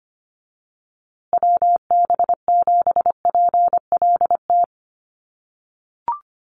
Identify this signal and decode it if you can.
Morse code, heard as a keyed tone, about 25 wpm, 700 Hz. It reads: WB7PLT
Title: USA Call Signs Repeated 3x 25wpm